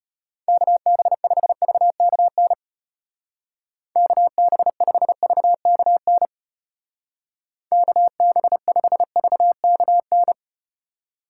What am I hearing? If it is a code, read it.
Morse code, heard as a keyed tone, about 34 wpm, 700 Hz. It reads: KB5VKN KB5VKN KB5VKN